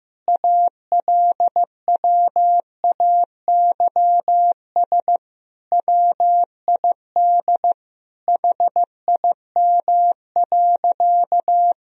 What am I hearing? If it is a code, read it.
Morse code, heard as a keyed tone, about 15 wpm, 700 Hz. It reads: ALWAYS WID HIM.
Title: Sentences from Top 400 Words and Abbreviations 15wpm